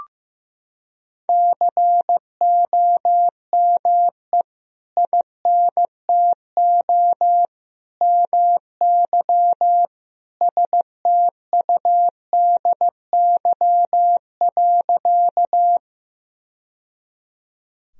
Morse code, heard as a keyed tone, about 15 wpm, 700 Hz. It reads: COME INTO MY STUDY.
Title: Sentences from Top 1000 Words - Review Part 1 - 15wpm